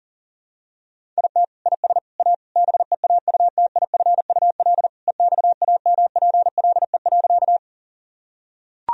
Morse code, heard as a keyed tone, about 40 wpm, 700 Hz. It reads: IT IS A BEAUTIFUL EXAMPLE.
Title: Sentences from Top 1000 Words - Review Part 1 - 40wpm